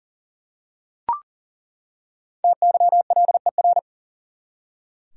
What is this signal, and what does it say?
Morse code, heard as a keyed tone, about 40 wpm, 700 Hz. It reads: TYLER
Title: USA Common Names - Max 5 Letters - 40wpm